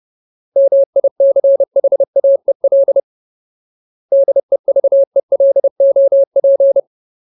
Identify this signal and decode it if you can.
Morse code, heard as a keyed tone, about 30 wpm, 550 Hz. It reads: MICHAEL DEVELOP
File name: Ultimate Word List / 30WPM Ultimate